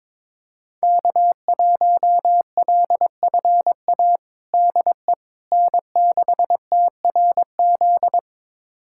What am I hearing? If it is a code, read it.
Morse code, heard as a keyed tone, about 22 wpm, 700 Hz. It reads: K1LFA DE N6TRZ
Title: USA Callsign DE Callsign - 5 Characters - 22wpm